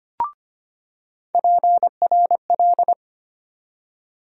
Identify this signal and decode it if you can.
Morse code, heard as a keyed tone, about 25 wpm, 700 Hz. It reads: PRL
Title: USA Call Sign Postfixes - 3 Letters 25wpm